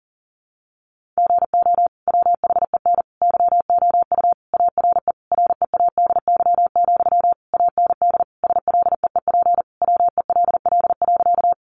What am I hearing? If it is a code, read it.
Morse code, heard as a keyed tone, about 40 wpm, 700 Hz. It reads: GO WHEN YOU ARE READY, AND SLEEP WELL.
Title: Sentences from Top 500 Words 40wpm